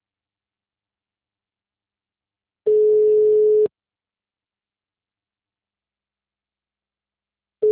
Ringback tone (Europe, 425 Hz).